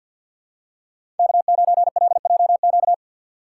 Morse code, heard as a keyed tone, about 50 wpm, 700 Hz. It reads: K9LJQ
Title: USA Call Signs - 5 Characters - 3x Character Spacing - 50wpm